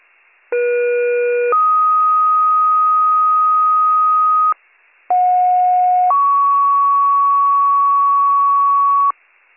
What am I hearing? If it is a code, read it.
Two-tone page, heard as a one-second tone, then a three-second tone, about 480 then 1250 Hz, about 730 then 1120 Hz. It is heard from 0.5 to 9.1 s.